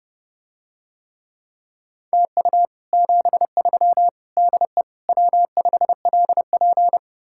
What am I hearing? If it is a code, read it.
Morse code, heard as a keyed tone, about 30 wpm, 700 Hz. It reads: TU 73 DE W5LP